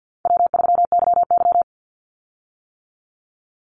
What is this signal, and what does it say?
Morse code, heard as a keyed tone, about 50 wpm, 700 Hz. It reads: W3YY
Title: Call Signs - 50wpm